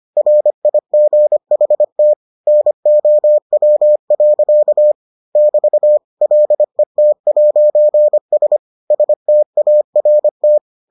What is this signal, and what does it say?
Morse code, heard as a keyed tone, about 25 wpm, 600 Hz. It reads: RIGHT NOW. <BT> LET'S START